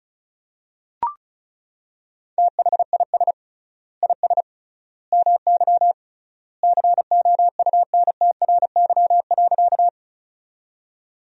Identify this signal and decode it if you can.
Morse code, heard as a keyed tone, about 36 wpm, 700 Hz. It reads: THIS IS MY COUNTRY.
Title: 3x Word-Spacing - Sentences from Top 300 Words - 35wpm